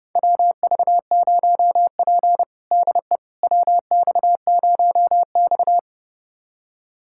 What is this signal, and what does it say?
Morse code, heard as a keyed tone, about 32 wpm, 700 Hz. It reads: WV0P DE WX0X
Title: USA Callsign DE Callsign - 4 Characters - 30wpm